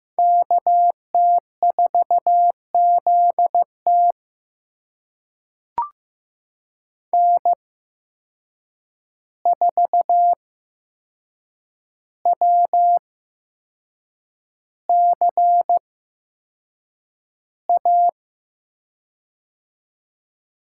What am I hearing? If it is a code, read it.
Morse code, heard as a keyed tone, about 15 wpm, spaced as for 6 wpm, 700 Hz. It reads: KT4ZT N4WCA
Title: USA Call Signs - 5 Characters - 8x Character Spacing - 15wpm